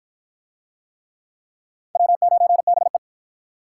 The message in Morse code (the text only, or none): W9BE